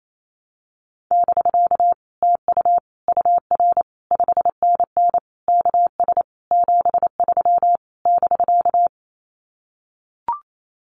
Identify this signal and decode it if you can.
Morse code, heard as a keyed tone, about 28 wpm, 700 Hz. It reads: <BK> TU UR 5NN KS 73 <BK>
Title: Common QSO Phrases Encore 28wpm